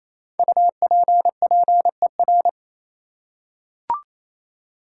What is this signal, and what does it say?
Morse code, heard as a keyed tone, about 28 wpm, 700 Hz. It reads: UPPER
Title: Five Letter Words - 28wpm